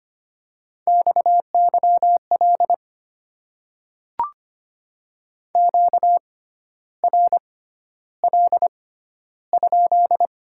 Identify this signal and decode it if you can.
Morse code, heard as a keyed tone, about 25 wpm, spaced as for 12 wpm, 690 Hz. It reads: XYL QRL?
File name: Top 90 QSO Elements - 6x Character Spacing - 25wpm